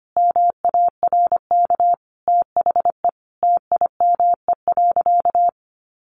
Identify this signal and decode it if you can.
Morse code, heard as a keyed tone, about 25 wpm, 700 Hz. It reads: MARK THE TIME.
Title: Sentences from Top 1400 Words 25wpm